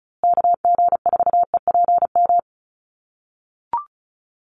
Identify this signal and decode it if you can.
Morse code, heard as a keyed tone, about 34 wpm, 700 Hz. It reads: KG4EPM
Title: USA Call Signs Repeated 2x 35wpm